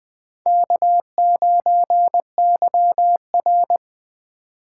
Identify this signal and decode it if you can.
Morse code, heard as a keyed tone, about 21 wpm, 700 Hz. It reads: K9YR